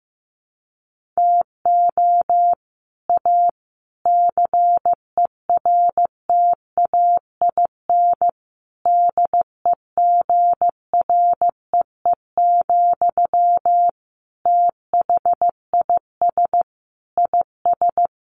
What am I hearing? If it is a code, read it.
Morse code, heard as a keyed tone, about 15 wpm, 700 Hz. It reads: TO A CERTAIN DEGREE, THIS IS
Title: Sentences from Top 1500 Words 15wpm